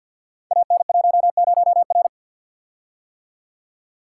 Morse code, heard as a keyed tone, about 50 wpm, 700 Hz. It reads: AN10R
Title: Code Groups - 5 Letter:Numbers - 50wpm